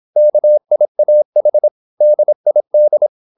Morse code, heard as a keyed tone, about 26 wpm, 600 Hz. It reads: KIAH DID
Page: Never 26wpm | CW med Gnister